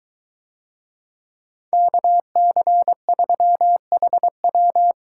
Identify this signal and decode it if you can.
Morse code, heard as a keyed tone, about 23 wpm, 700 Hz. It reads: KC3HW